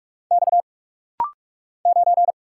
Morse code, heard as a keyed tone, about 46 wpm, 700 Hz. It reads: X 9
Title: Single Letter-Number Alt - Rapid-Fire 45wpm